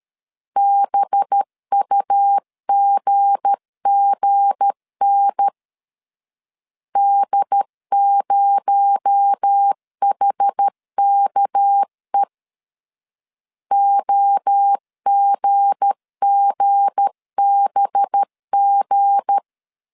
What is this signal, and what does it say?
Morse code, heard as a keyed tone, about 13 wpm, 800 Hz. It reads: BUGGN D0HKE OGGBG
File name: LEZIONI / 7 GF9Z / 7 G 12wpm 2055